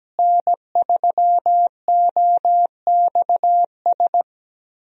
Morse code, heard as a keyed tone, about 17 wpm, 700 Hz. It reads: N3OXS
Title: USA Call Signs - 5 Characters - Repeated 2x - 17wpm